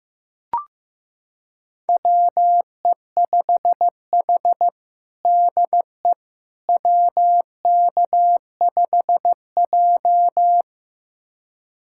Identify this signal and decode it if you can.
Morse code, heard as a keyed tone, about 16 wpm, 700 Hz. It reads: WE5H DE WK5J
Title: USA Callsign DE Callsign - 4 Characters - 15wpm